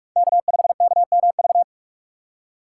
Morse code, heard as a keyed tone, about 46 wpm, 700 Hz. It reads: K5KMV
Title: USA Call Signs - 3x Character Spacing - 45wpm